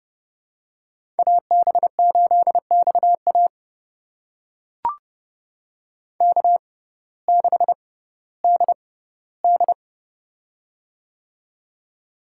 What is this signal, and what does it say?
Morse code, heard as a keyed tone, about 30 wpm, spaced as for 17 wpm, 700 Hz. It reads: AB8XA K6DD
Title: USA Call Signs - 6x Character Spacing - 30wpm